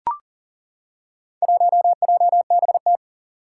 Morse code, heard as a keyed tone, about 40 wpm, 700 Hz. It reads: 1JBT
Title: Four Challenging Letter-Number Combinations 40wpm